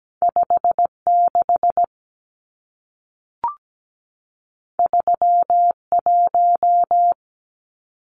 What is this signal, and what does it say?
Morse code, heard as a keyed tone, about 17 wpm, 700 Hz. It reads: 56 31
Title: Numbers - 2-Digit 17wpm